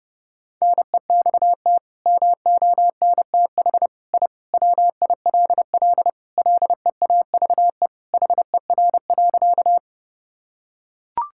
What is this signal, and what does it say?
Morse code, heard as a keyed tone, about 32 wpm, 700 Hz. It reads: NEXT MONTH I WILL LEAVE HER.
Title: Sentences from Top 1000 Words - Review Part 5 - 30wpm